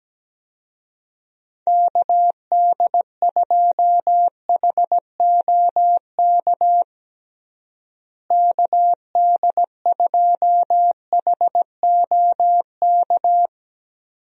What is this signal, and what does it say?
Morse code, heard as a keyed tone, about 17 wpm, 700 Hz. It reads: KD2HOK KD2HOK
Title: USA Call Signs - 6 Characters - Repeated 2x - 17wpm